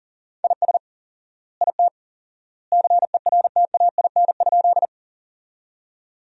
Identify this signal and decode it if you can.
Morse code, heard as a keyed tone, about 40 wpm, 700 Hz. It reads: IS IT CERTAIN?
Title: 4x Word-Spacing - Sentences from Top 500 Words - 40wpm